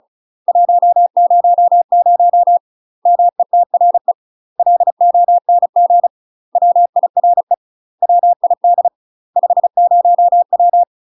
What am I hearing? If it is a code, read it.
Morse code, heard as a keyed tone, about 36 wpm, 700 Hz. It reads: 100 METRE LONG WIRE WID 50W
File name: UK MASTER QSOs TRAINING / 35WPM / UK MASTER QSOs 0014